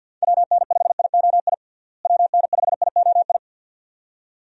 Morse code, heard as a keyed tone, about 50 wpm, 700 Hz. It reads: WN5IOI WN5IOI
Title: USA Call Signs - 6 Characters - Repeated 2x - 50wpm